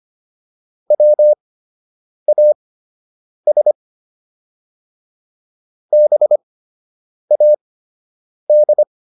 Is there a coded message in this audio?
Yes: Morse code from 0.9 to 8.8 s, heard as a keyed tone, about 25 wpm, spaced as for 8 wpm, 600 Hz. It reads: WAS BAD